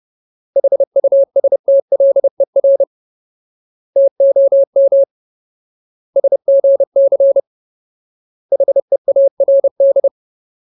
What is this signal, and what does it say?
Morse code, heard as a keyed tone, about 30 wpm, 550 Hz. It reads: HUSTLER TOM SGC HEARD
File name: Ultimate Word List / 30WPM Ultimate